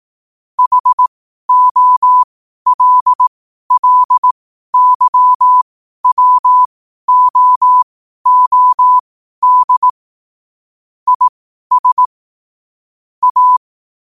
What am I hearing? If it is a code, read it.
Morse code, heard as a keyed tone, about 18 wpm, spaced as for 12 wpm, 1000 Hz. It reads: HOLLYWOOD IS A
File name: qotdfiles / 12wpm